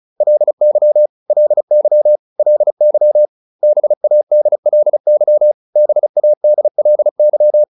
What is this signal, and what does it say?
Morse code, heard as a keyed tone, about 34 wpm, 600 Hz. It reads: LY LY LY BADLY BADLY
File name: LY5 35